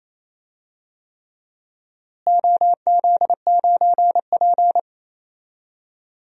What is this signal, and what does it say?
Morse code, heard as a keyed tone, about 27 wpm, 700 Hz. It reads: OZ9P
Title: DXCC-100 Call Signs - 4 Characters - Repeated 3x - 28wpm